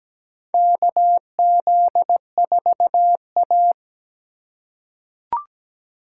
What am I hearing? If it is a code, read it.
Morse code, heard as a keyed tone, about 17 wpm, 700 Hz. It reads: KZ4A
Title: K1USN SST QSO Practice - 17wpm